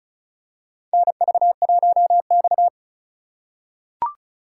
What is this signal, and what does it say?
Morse code, heard as a keyed tone, about 34 wpm, 700 Hz. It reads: NV1X
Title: USA Call Signs - 4 Characters - 4x Character Spacing - 35wpm